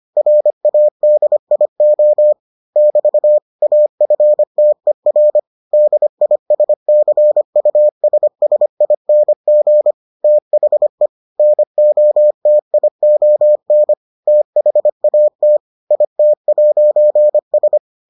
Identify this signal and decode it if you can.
Morse code, heard as a keyed tone, about 25 wpm, 600 Hz. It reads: RADIO <BT> AFTER DISCUSSING THE NOTION THAT IT'S